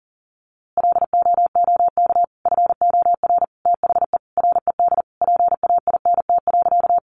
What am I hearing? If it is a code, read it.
Morse code, heard as a keyed tone, about 40 wpm, 700 Hz. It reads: LOOK FOR THE RED PAINT.
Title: A Sound Of Thunder 40wpm